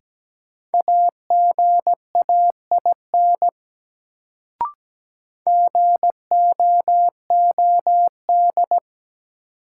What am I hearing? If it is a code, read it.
Morse code, heard as a keyed tone, about 17 wpm, 700 Hz. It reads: AGAIN GOOD
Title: Sentences from Top 200 Words - New Word Drill - 17wpm